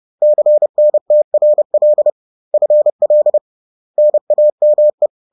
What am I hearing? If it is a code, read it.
Morse code, heard as a keyed tone, about 30 wpm, 600 Hz. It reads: CNTRL FL NAME